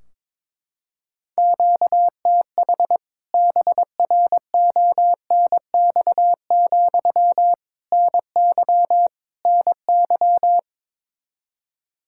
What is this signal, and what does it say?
Morse code, heard as a keyed tone, about 22 wpm, 700 Hz. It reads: QTH BRONX, NY NY
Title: K1USN SST QSO Practice 22wpm